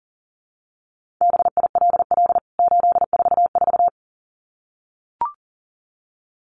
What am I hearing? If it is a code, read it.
Morse code, heard as a keyed tone, about 40 wpm, 700 Hz. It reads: BILL 844
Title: CWT Exchange 40wpm